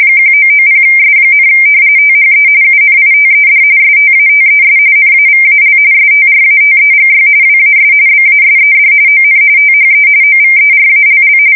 RTTY, about 45.45 baud. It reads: 1990S, THE UNITED STATES NAVAL STRATEGY WAS BASED ON THE OVERALL
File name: fsk170 / usnavy5